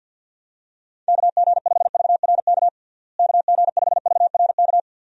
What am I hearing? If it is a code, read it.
Morse code, heard as a keyed tone, about 51 wpm, 700 Hz. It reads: KG5VRK KG5VRK
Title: USA Call Signs - 6 Characters - Repeated 2x - 50wpm